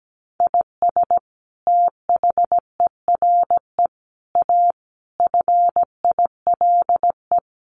Morse code, heard as a keyed tone, about 17 wpm, 700 Hz. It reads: IS THERE A FILE
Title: Sentences from Top 1200 Words 17wpm